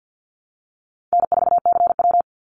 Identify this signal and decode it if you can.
Morse code, heard as a keyed tone, about 50 wpm, 700 Hz. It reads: N4CW
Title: Call Signs - 50wpm